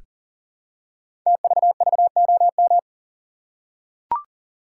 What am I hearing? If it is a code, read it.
Morse code, heard as a keyed tone, about 43 wpm, 700 Hz. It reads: TVVOM
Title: Code Groups - 5 Letters - 40wpm